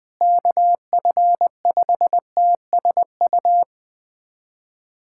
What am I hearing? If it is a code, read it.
Morse code, heard as a keyed tone, about 20 wpm, 700 Hz. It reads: KF5TSU
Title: USA Callsign DE Callsign - 6 Characters - 20wpm